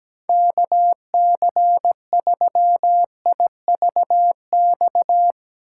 Morse code, heard as a keyed tone, about 17 wpm, 700 Hz. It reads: KC3IVX
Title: USA Call Signs - 6 Characters - 4x Character Spacing - 17wpm